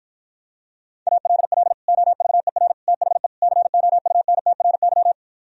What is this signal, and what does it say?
Morse code, heard as a keyed tone, about 53 wpm, 700 Hz. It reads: ALL OVER THE COUNTRY